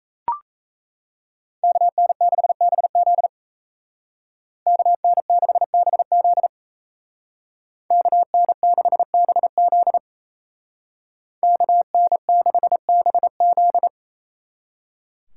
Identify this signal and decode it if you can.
Morse code, heard as a keyed tone, about 36 wpm, 700 Hz. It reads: KN6BZ KN6BZ KN6BZ KN6BZ